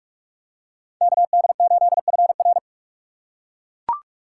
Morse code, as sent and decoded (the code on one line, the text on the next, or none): -.- -.. ---.. ..-. .-.
KD8FR